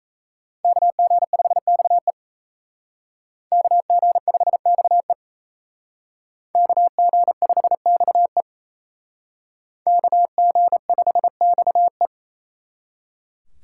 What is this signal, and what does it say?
Morse code, heard as a keyed tone, about 35 wpm, 700 Hz. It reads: KG5XE KG5XE KG5XE KG5XE
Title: Speed Racing - USA Call Signs 28wpm